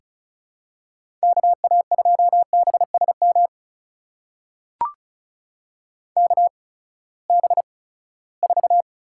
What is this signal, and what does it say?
Morse code, heard as a keyed tone, about 36 wpm, spaced as for 20 wpm, 700 Hz. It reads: KA2BSM KB4
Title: USA Call Signs - 6 Characters - 8x Character Spacing - 35wpm